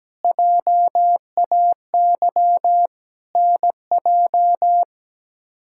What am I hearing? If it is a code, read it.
Morse code, heard as a keyed tone, about 17 wpm, 700 Hz. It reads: JAY NJ
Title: K1USN SST QSO Practice - 17wpm